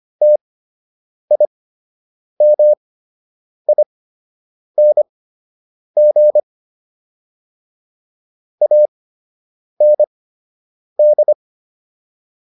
Morse code, heard as a keyed tone, about 25 wpm, spaced as for 8 wpm, 600 Hz. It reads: TIMING AND